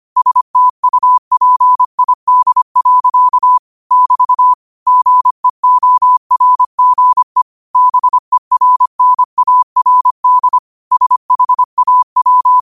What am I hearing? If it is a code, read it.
Morse code, heard as a keyed tone, about 25 wpm, 1000 Hz. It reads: STUPID. <BT> GEORGE BERNARD SHAW